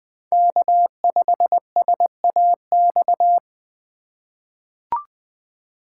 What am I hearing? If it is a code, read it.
Morse code, heard as a keyed tone, about 20 wpm, 700 Hz. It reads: K5SAX
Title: USA Call Signs - 3x Character Spacing - 20wpm